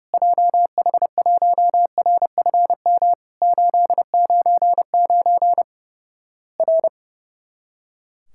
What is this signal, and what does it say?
Morse code, heard as a keyed tone, about 30 wpm, 700 Hz. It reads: JH1RFM 899 R
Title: Worked All Europe DX Contest - QTC Exchange - 30wpm